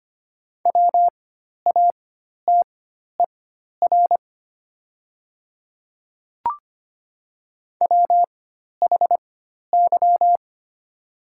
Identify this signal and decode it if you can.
Morse code, heard as a keyed tone, about 25 wpm, spaced as for 12 wpm, 700 Hz. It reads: WATER WHY
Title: 4x Character Spacing - Top 200 Words - 25wpm